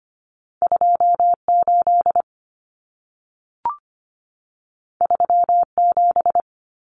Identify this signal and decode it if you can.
Morse code, heard as a keyed tone, about 25 wpm, 700 Hz. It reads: 28 37
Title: Numbers - 2-Digit 25wpm